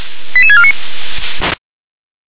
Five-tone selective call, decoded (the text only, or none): 88649